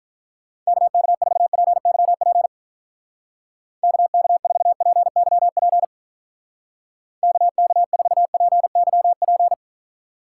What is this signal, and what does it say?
Morse code, heard as a keyed tone, about 43 wpm, 700 Hz. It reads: KK4PYP KK4PYP KK4PYP
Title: Speed Racing - USA Call Signs 35wpm